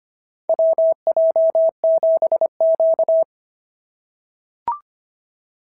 Morse code, heard as a keyed tone, about 25 wpm, 650 Hz. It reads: WJ7Q
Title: SOTA QSO Practice 25wpm